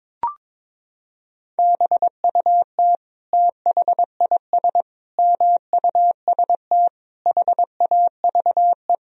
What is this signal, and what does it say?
Morse code, heard as a keyed tone, about 22 wpm, 700 Hz. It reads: BUT THIS MUST HAVE
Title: Sentences from Top 1000 Words - Review Part 3 - 22wpm